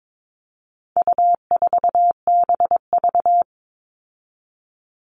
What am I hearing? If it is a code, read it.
Morse code, heard as a keyed tone, about 22 wpm, 700 Hz. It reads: U4BV
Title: All Cut Numbers - 4-Digit 22wpm